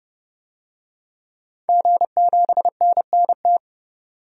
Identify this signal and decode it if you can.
Morse code, heard as a keyed tone, about 30 wpm, 700 Hz. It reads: G7NNT
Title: DXCC-100 Call Signs - 5 Characters - 30wpm